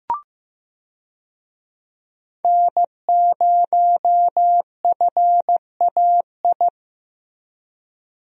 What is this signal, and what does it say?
Morse code, heard as a keyed tone, about 16 wpm, 700 Hz. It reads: N0FAI
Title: USA Call Signs - 5 Characters - Repeated 3x - 15wpm